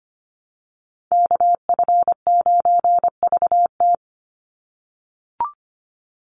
Morse code, heard as a keyed tone, about 25 wpm, 700 Hz. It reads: KF9VT
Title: USA Call Signs 25wpm